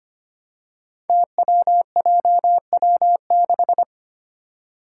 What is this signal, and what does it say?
Morse code, heard as a keyed tone, about 25 wpm, 700 Hz. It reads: TWJW6
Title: Five Challenging Letter-Number Combinations 25wpm